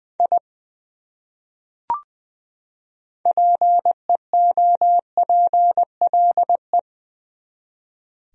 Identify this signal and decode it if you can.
Morse code, heard as a keyed tone, about 20 wpm, 700 Hz. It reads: I PEOPLE
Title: Top 100 Words 20wpm